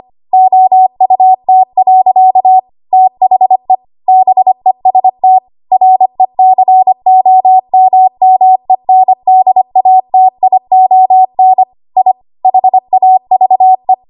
Morse code, heard as a keyed tone, about 25 wpm, 750 Hz. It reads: OUT. THE BEST RECOMMENDATION I HAVE
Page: 25 WPM Code Archive